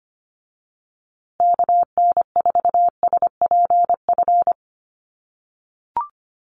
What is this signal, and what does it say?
Morse code, heard as a keyed tone, about 25 wpm, 700 Hz. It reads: KN4SPF
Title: USA Call Signs Repeated 3x 25wpm